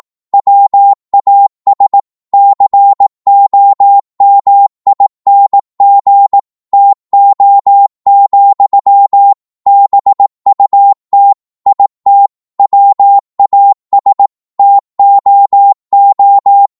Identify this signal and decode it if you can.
Morse code, heard as a keyed tone, about 18 wpm, 800 Hz. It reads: WAS COMING TO, BUT IT WAS TOO
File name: Morse / psg-18wpm